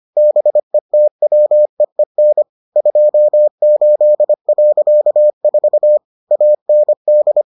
Morse code, heard as a keyed tone, about 25 wpm, 600 Hz. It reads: BETWEEN 28.4 AND